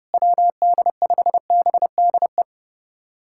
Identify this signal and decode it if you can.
Morse code, heard as a keyed tone, about 30 wpm, 700 Hz. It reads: WD5BDE
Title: USA Callsign DE Callsign 30wpm